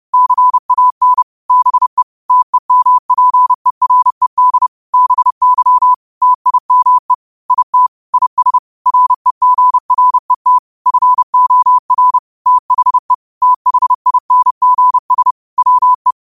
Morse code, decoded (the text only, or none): CAN BE TEMPERED BY TIME IT IS REGRET FOR THE THINGS WE